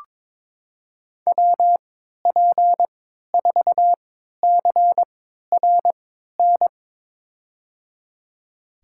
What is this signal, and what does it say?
Morse code, heard as a keyed tone, about 22 wpm, spaced as for 13 wpm, 700 Hz. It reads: WP4CRN
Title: USA Call Signs - 6 Characters - 3x Character Spacing - 22wpm